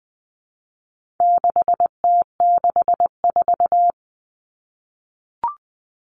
Morse code, heard as a keyed tone, about 20 wpm, 700 Hz. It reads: 6T64
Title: All Cut Numbers - 4-Digit 20wpm